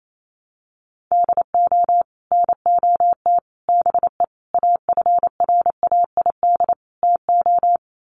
Morse code, heard as a keyed tone, about 28 wpm, 700 Hz. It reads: DO NOT BE AFRAID TO
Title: Sets of 5 Words Encore2 28wpm